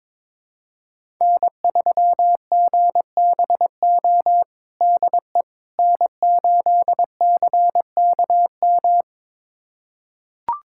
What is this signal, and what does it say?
Morse code, heard as a keyed tone, about 22 wpm, 700 Hz. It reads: N3GBO DE N8CKM
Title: USA Callsign DE Callsign - 5 Characters - 22wpm